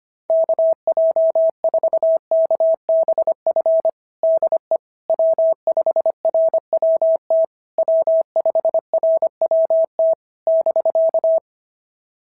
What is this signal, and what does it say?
Morse code, heard as a keyed tone, about 24 wpm, 650 Hz. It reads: KJ4KBF DE W5RWT W5RWT <BK>
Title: SOTA QSO Practice 25wpm